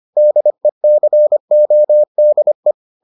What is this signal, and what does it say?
Morse code, heard as a keyed tone, about 25 wpm, 600 Hz. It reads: DECODE